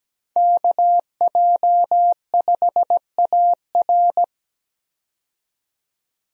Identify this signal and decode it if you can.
Morse code, heard as a keyed tone, about 17 wpm, 700 Hz. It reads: KJ5AR
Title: USA Call Signs - 5 Characters - 17wpm